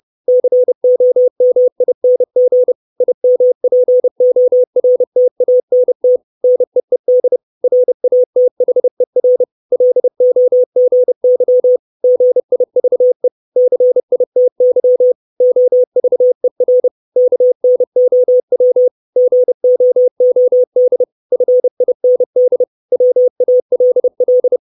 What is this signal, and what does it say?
Morse code, heard as a keyed tone, about 30 wpm, 500 Hz. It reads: COMING IMPORTANT NEED RATHER LOGY GIVE CITY OVER KNOW GOOD FIND WALL
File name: 632 Words 30WPM